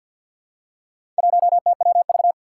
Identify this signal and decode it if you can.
Morse code, heard as a keyed tone, about 52 wpm, 700 Hz. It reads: 1TWV
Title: Four Challenging Letter-Number Combinations 50wpm